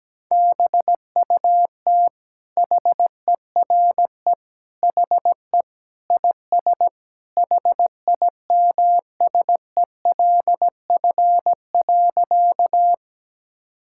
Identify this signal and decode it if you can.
Morse code, heard as a keyed tone, about 17 wpm, 700 Hz. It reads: BUT HERE HE IS HIMSELF.
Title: Sentences from Top 1600 Words 17wpm